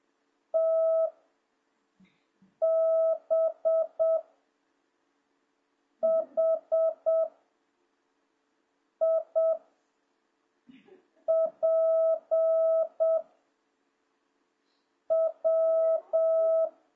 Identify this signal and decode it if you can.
Morse code, heard as a keyed tone, about 7 wpm, 650 Hz. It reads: T B H I P W